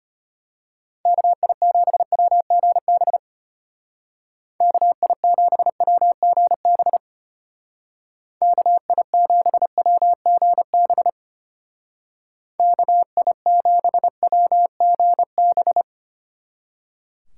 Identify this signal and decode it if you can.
Morse code, heard as a keyed tone, about 32 wpm, 700 Hz. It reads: KI7WGB KI7WGB KI7WGB KI7WGB